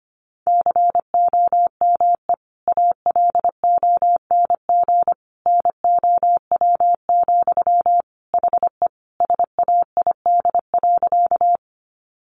Morse code, heard as a keyed tone, about 25 wpm, 700 Hz. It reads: COME ALONG NOW, HE SAID.
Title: Sentences from Top 700 Words 25wpm